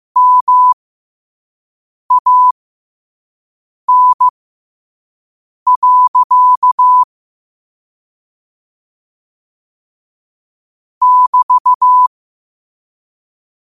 Morse code, heard as a keyed tone, about 15 wpm, spaced as for 5 wpm, 1000 Hz. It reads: MAN. <BT>
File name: qotdfiles / 5wpm